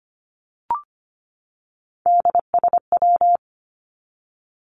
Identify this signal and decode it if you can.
Morse code, heard as a keyed tone, about 25 wpm, 700 Hz. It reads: DSW
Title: Abbreviations 25wpm